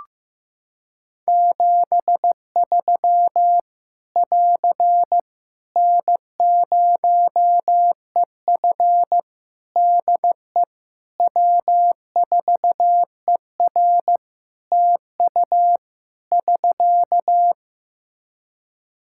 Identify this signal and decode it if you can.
Morse code, heard as a keyed tone, about 15 wpm, 700 Hz. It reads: 73 <AR> N0EF DE W4ER TU <SK>
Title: Practice QSO - Exchange 4 - Ending - 15wpm